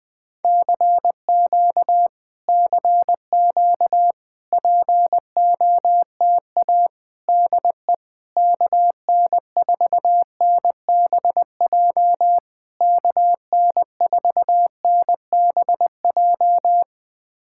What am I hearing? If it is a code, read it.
Morse code, heard as a keyed tone, about 20 wpm, 700 Hz. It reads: CQ CQ POTA DE KN4NBJ KN4NBJ
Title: POTA QSO Practice - 20wpm